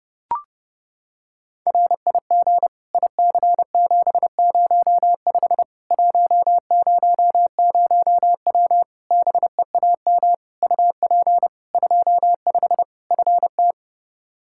Morse code, heard as a keyed tone, about 30 wpm, 700 Hz. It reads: RIG IC705 100W BEAM UP 25 FT
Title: K1USN SST QSO Practice 30wpm